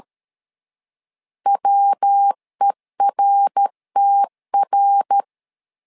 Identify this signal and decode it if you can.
Morse code, heard as a keyed tone, about 13 wpm, 800 Hz. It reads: WERTR
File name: LEZIONI / 4 Q1S3 12wpm 2023